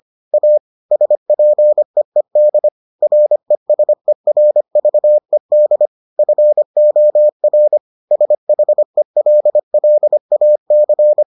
Morse code, heard as a keyed tone, about 25 wpm, 600 Hz. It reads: A SPEED RESERVED FOR SHELLAC